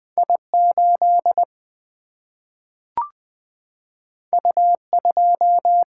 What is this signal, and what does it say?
Morse code, heard as a keyed tone, about 20 wpm, 700 Hz. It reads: I8 U2
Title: Single-Letter Single-Number 20wpm